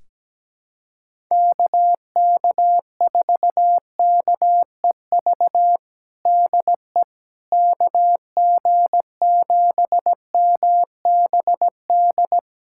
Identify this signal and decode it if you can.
Morse code, heard as a keyed tone, about 17 wpm, 700 Hz. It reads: KK4KEV DE KG7MBD
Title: USA Callsign DE Callsign - 6 Characters - 17wpm